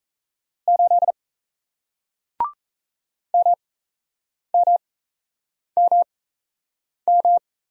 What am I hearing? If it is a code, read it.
Morse code, heard as a keyed tone, about 38 wpm, spaced as for 8 wpm, 700 Hz. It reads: 8 MMMM